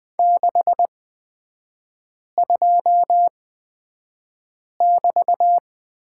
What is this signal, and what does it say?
Morse code, heard as a keyed tone, about 20 wpm, spaced as for 5 wpm, 700 Hz. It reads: 62<BT>
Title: 224 Lesson 36e - BT - Three Characters 20wpm